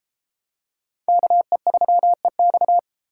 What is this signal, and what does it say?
Morse code, heard as a keyed tone, about 32 wpm, 700 Hz. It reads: KE3EX